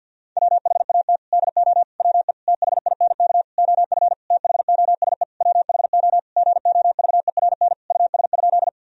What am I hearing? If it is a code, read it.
Morse code, heard as a keyed tone, about 50 wpm, 700 Hz. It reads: WHAT DO WE THINK OF THOSE WHO GOVERN US?
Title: Sentences from Top 1000 Words - Review Part 2 - 50wpm